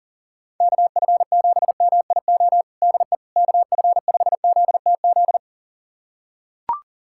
Morse code, heard as a keyed tone, about 40 wpm, 700 Hz. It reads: KF7MIO DE KF5ZTZ
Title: USA Callsign DE Callsign 40wpm